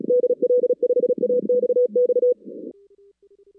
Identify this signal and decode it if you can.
Morse code, heard as a keyed tone, about 37 wpm, 490 Hz. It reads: DL5AXX
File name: 2016 arrl dx cw / 2016-ARRL-DX-CW-K5TR- - 0-41